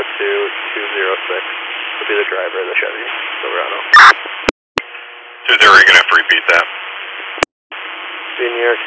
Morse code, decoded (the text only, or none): none